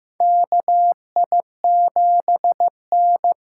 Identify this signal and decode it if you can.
Morse code, heard as a keyed tone, about 15 wpm, 700 Hz. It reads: KI7N
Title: USA Call Signs - 6 Characters - 8x Character Spacing - 15wpm